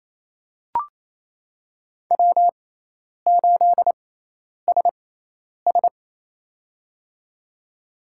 Morse code, heard as a keyed tone, about 28 wpm, spaced as for 10 wpm, 700 Hz. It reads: W8SS
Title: USA Call Signs - 4 Characters - 6x Character Spacing - 28wpm